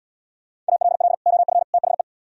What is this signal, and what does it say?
Morse code, heard as a keyed tone, about 76 wpm, 700 Hz. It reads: ALL OF THE